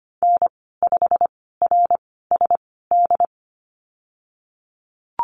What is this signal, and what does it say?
Morse code, heard as a keyed tone, about 25 wpm, spaced as for 16 wpm, 700 Hz. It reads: N5RSD
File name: Lesson 15g - Number 5 - Call Signs 25wpm